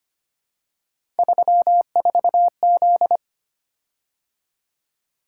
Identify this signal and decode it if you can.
Morse code, heard as a keyed tone, about 25 wpm, 700 Hz. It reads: 34Z
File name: Three Letter-Number Combinations - 25wpm